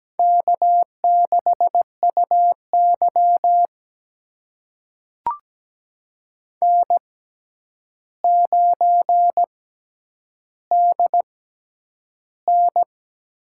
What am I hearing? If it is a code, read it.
Morse code, heard as a keyed tone, about 17 wpm, spaced as for 9 wpm, 700 Hz. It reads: K6UY N9DN
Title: USA Call Signs - 4 Characters - 6x Character Spacing - 17wpm